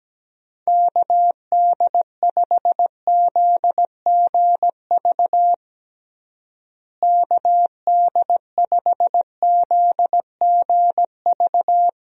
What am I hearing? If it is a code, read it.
Morse code, heard as a keyed tone, about 17 wpm, 700 Hz. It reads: KD5ZGV KD5ZGV
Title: USA Call Signs - 6 Characters - Repeated 2x - 17wpm